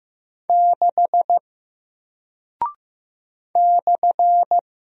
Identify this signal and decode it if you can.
Morse code, heard as a keyed tone, about 15 wpm, 700 Hz. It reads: 6 /
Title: Single Character (Letter:Number:Punctuation:Prosign) Alt 15wpm